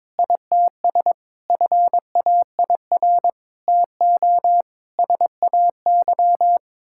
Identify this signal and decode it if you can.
Morse code, heard as a keyed tone, about 21 wpm, 700 Hz. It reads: ITS FAIR TO SAY
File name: practice / 22wpm / Sets of 6 Words - 22wpm